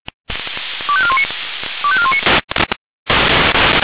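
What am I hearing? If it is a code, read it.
Five-tone selective call (ZVEI), heard as quick runs of five tones, tones of about 70 ms. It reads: 25419 25419
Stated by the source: ZVEI five-tone selective call 25218 received on SDR.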